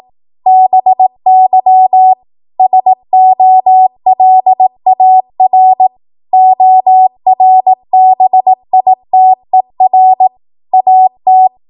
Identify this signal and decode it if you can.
Morse code, heard as a keyed tone, about 18 wpm, 750 Hz. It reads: BY SOLAR ORBITER AT
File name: Morse / 18WPM